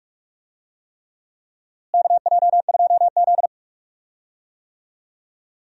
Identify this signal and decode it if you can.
Morse code, heard as a keyed tone, about 44 wpm, 700 Hz. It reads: KJ2Z
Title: USA Call Signs - 4 Characters - Repeated 3x - 45wpm